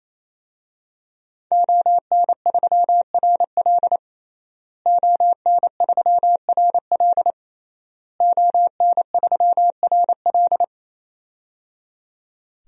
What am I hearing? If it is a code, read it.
Morse code, heard as a keyed tone, about 28 wpm, 700 Hz. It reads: ON3RL ON3RL ON3RL
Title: DXCC-100 Call Signs - 5 Characters - Repeated 3x - 28wpm